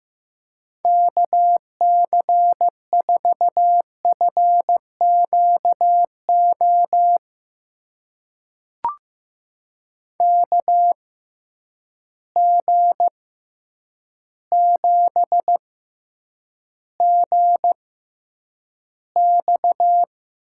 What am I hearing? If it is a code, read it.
Morse code, heard as a keyed tone, about 15 wpm, spaced as for 8 wpm, 700 Hz. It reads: KC4FQO KG7GX
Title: USA Call Signs - 6 Characters - 6x Character Spacing - 15wpm